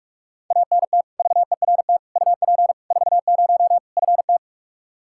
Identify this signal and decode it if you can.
Morse code, heard as a keyed tone, about 46 wpm, 700 Hz. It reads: ANT VERT UP 40 FT
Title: Practice QSO - Exchange 2 - Info - 45wpm